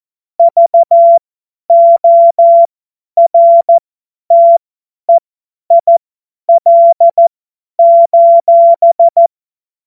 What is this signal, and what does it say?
Morse code, heard as a keyed tone, about 14 wpm, spaced as for 10 wpm, 680 Hz. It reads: VORTEIL:
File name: Morsekurs / Witze / Witz 02. August 2022 50 F70